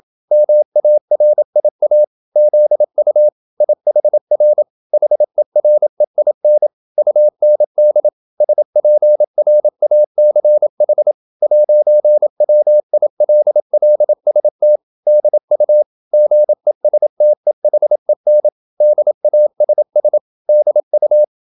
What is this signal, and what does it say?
Morse code, heard as a keyed tone, about 27 wpm, 600 Hz. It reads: MARIA ZU IHR HEREIN UND SPRACH 'WILLST DU GESTEHEN DASS DU